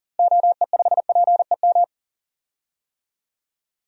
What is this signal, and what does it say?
Morse code, heard as a keyed tone, about 40 wpm, 700 Hz. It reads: OE5PEM
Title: DXCC-100 Call Signs - 6 Characters - 40wpm